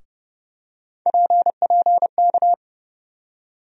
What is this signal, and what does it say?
Morse code, heard as a keyed tone, about 30 wpm, 700 Hz. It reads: PPK
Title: USA Call Sign Postfixes - 3 Letters 30wpm